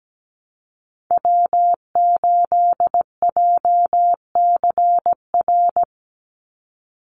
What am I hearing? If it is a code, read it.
Morse code, heard as a keyed tone, about 17 wpm, 700 Hz. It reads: W8JCR
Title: USA Call Signs 17wpm